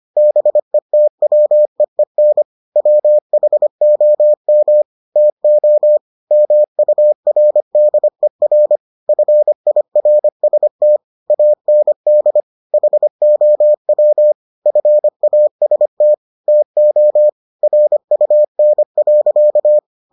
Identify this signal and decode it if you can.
Morse code, heard as a keyed tone, about 25 wpm, 600 Hz. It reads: BETWEEN WHOM TO MURDER FIRST AND HOW FAST TO RUN.